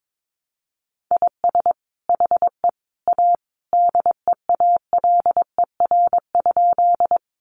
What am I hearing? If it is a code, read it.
Morse code, heard as a keyed tone, about 22 wpm, 700 Hz. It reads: IS HE A DEALER?
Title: Sentences from Top 1100 Words 22wpm